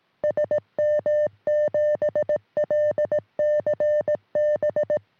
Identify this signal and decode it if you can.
Morse code, heard as a keyed tone, about 18 wpm, 600 Hz. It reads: SM7LCB
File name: sm7lcb / g4bao all 23cm